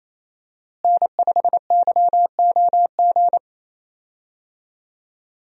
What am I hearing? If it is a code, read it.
Morse code, heard as a keyed tone, about 28 wpm, 700 Hz. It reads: N5YOG
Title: USA Call Signs - 5 Characters - Repeated 2x - 28wpm